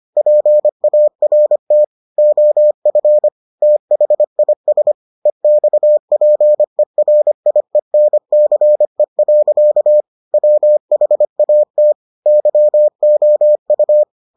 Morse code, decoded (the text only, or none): PART OF THIS EXPERIENCE. WHAT YOU